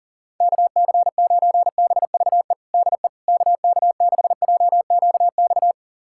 Morse code, heard as a keyed tone, about 40 wpm, 700 Hz. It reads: KC9BVE DE KK6JQX
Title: USA Callsign DE Callsign - 6 Characters - 40wpm